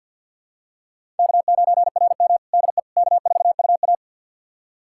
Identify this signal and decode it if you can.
Morse code, heard as a keyed tone, about 52 wpm, 700 Hz. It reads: K9RM DE K4UA
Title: USA Callsign DE Callsign - 4 Characters - 50wpm